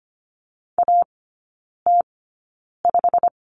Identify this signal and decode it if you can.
Morse code, heard as a keyed tone, about 25 wpm, spaced as for 9 wpm, 700 Hz. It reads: AT5
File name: course / 25:10wpm / 093 Lesson 16e - Letter c - Three Characters 25wpm